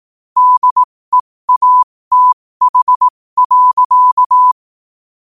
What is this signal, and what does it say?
Morse code, heard as a keyed tone, about 18 wpm, spaced as for 15 wpm, 1000 Hz. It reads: DEATH.
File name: qotdfiles / 15wpm